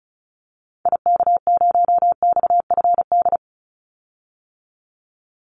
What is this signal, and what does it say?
Morse code, heard as a keyed tone, about 34 wpm, 700 Hz. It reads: IK0XFD
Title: Call Signs - 35wpm